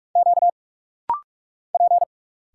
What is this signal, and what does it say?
Morse code, heard as a keyed tone, about 46 wpm, 700 Hz. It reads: Q P
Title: Single Letters - Rapid-Fire 45wpm